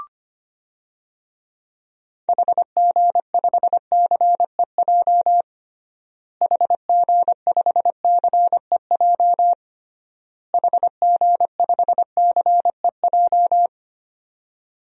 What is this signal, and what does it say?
Morse code, heard as a keyed tone, about 25 wpm, 700 Hz. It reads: HG5CEJ HG5CEJ HG5CEJ
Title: DXCC-100 Call Signs - 6 Characters - Repeated 3x - 25wpm